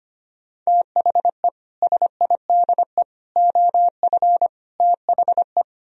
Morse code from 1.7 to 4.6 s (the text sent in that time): SIDE OF